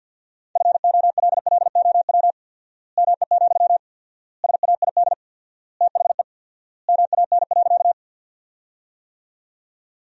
Morse code, heard as a keyed tone, about 50 wpm, 700 Hz. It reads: FOLLOW ME, SAID THE MAN.
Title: 4x Word-Spacing - Sentences from Top 200 Words - 50wpm